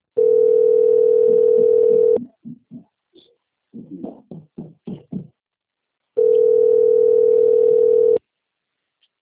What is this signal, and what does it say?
Ringback tone (North America), 440 + 480 Hz.